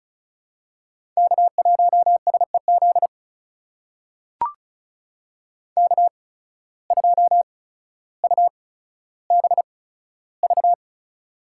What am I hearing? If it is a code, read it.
Morse code, heard as a keyed tone, about 34 wpm, spaced as for 15 wpm, 700 Hz. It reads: K1SEZ K2UBV
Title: USA Call Signs - 5 Characters - 8x Character Spacing - 35wpm